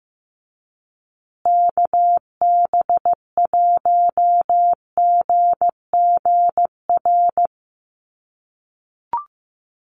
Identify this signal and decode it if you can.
Morse code, heard as a keyed tone, about 15 wpm, 700 Hz. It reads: KB1GGR
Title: USA Call Signs Repeated 3x 15wpm